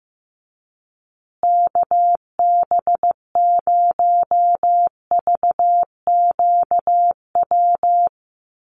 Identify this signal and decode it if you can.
Morse code, heard as a keyed tone, about 14 wpm, 700 Hz. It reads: KB0VQW
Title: USA Call Signs Repeated 2x 15wpm